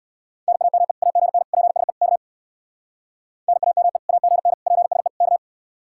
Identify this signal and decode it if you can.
Morse code, heard as a keyed tone, about 64 wpm, 700 Hz. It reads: NAME AGN PSE K NAME AGN PSE K